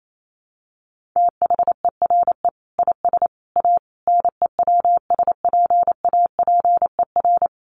Morse code, heard as a keyed tone, about 28 wpm, 700 Hz. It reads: THERE IS A NEWSPAPER
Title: Sentences from Top 1100 Words 28wpm